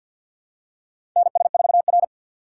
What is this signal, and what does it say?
Morse code, heard as a keyed tone, about 53 wpm, 700 Hz. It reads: NS4R
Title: USA Call Signs - 4 Characters - 2x Character Spacing - 50wpm